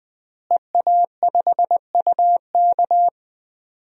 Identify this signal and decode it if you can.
Morse code, heard as a keyed tone, about 20 wpm, 700 Hz. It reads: EA5UK
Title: DXCC-100 Call Signs - 5 Characters - Repeated 3x - 20wpm